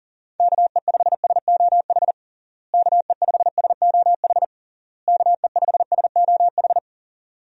Morse code, heard as a keyed tone, about 40 wpm, 700 Hz. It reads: KE5SOH KE5SOH KE5SOH
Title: USA Call Signs - 6 Characters - Repeated 3x - 40wpm